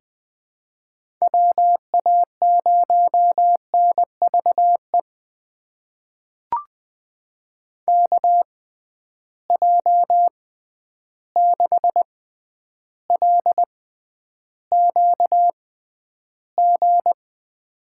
Morse code, heard as a keyed tone, about 21 wpm, spaced as for 10 wpm, 700 Hz. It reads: WA0NVE KJ6LQG
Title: USA Call Signs - 6 Characters - 6x Character Spacing - 20wpm